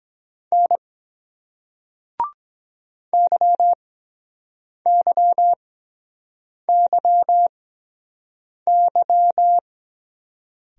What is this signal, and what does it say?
Morse code, heard as a keyed tone, about 23 wpm, spaced as for 7 wpm, 700 Hz. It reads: N YYYY